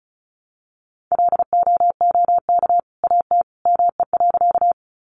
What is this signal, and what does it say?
Morse code, heard as a keyed tone, about 34 wpm, 700 Hz. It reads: LOOK AT ME.
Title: Sentences from Top 200 Words 35wpm